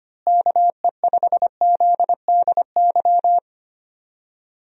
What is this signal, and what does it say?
Morse code, heard as a keyed tone, about 25 wpm, 700 Hz. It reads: KE5ZDY
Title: USA Call Signs - 6 Characters - Repeated 3x - 25wpm